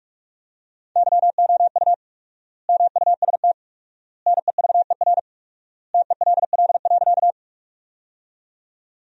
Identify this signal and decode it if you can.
Morse code, heard as a keyed tone, about 43 wpm, 700 Hz. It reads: YOU MUST NEVER TELL.
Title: 4x Word-Spacing - Sentences from Top 300 Words - 45wpm